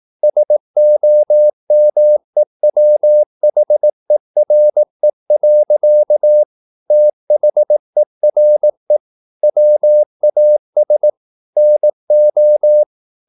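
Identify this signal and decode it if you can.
Morse code, heard as a keyed tone, about 18 wpm, 600 Hz. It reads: SOMEWHERE. THERE WAS NO
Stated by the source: Morse practice at 18 wpm, English text, from CW med Gnister.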